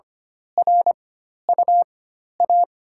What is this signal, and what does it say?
Morse code, heard as a keyed tone, about 25 wpm, spaced as for 12 wpm, 700 Hz. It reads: RUA
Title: USA Call Signs - 6 Characters - 4x Character Spacing - 25wpm